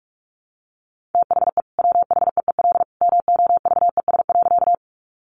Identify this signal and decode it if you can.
Morse code, heard as a keyed tone, about 44 wpm, 700 Hz. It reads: THE WHEEL MOVES.
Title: Additional Easy Phrases 45wpm